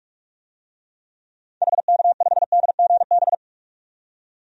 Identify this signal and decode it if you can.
Morse code, heard as a keyed tone, about 41 wpm, 700 Hz. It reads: HK5DGB
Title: DXCC-100 Call Signs - 6 Characters - 45wpm